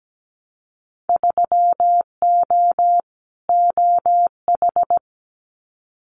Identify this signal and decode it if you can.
Morse code, heard as a keyed tone, about 17 wpm, 700 Hz. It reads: 3O OH
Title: Winter Field Day Exchanges 17wpm